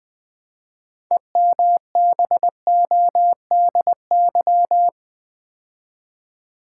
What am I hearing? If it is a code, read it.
Morse code, heard as a keyed tone, about 20 wpm, 700 Hz. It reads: EMBODY
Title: Build A Word - 2 Letters At A Time - 6-Letter Words - 20wpm